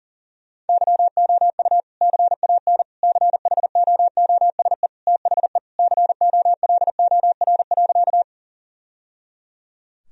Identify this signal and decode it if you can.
Morse code, heard as a keyed tone, about 40 wpm, 700 Hz. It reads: YOU CAN CHOOSE THE COLOR.
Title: Sentences from Top 1000 Words - Review Part 4 - 40wpm